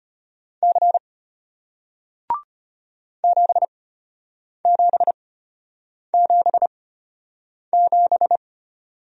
Morse code, heard as a keyed tone, about 33 wpm, spaced as for 8 wpm, 700 Hz. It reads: C 7777